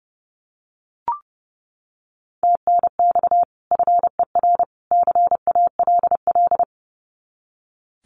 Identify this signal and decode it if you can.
Morse code, heard as a keyed tone, about 30 wpm, 700 Hz. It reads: TNX FER CALL
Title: Common QSO Phrases 30wpm